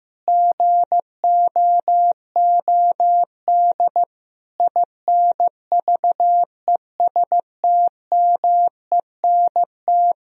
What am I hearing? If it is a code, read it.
Morse code, heard as a keyed tone, about 15 wpm, 700 Hz. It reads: GOOD INVESTMENT
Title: Sentences from Top 1000 Words - Review Part 4 - 15wpm